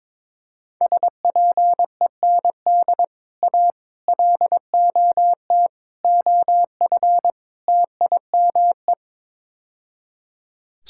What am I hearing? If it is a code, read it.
Morse code, heard as a keyed tone, about 22 wpm, 700 Hz. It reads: SPEND A LOT OF TIME
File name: practice / 22wpm / Build A Phrase - 5-Word Phrases - 22wpm